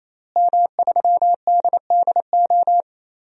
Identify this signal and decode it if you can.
Morse code, heard as a keyed tone, about 28 wpm, 700 Hz. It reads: M3DDO
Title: DXCC-100 Call Signs - 5 Characters - Repeated 3x - 28wpm